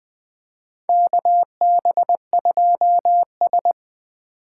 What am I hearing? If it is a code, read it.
Morse code, heard as a keyed tone, about 20 wpm, 700 Hz. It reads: KB2S
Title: K1USN SST QSO Practice - 20wpm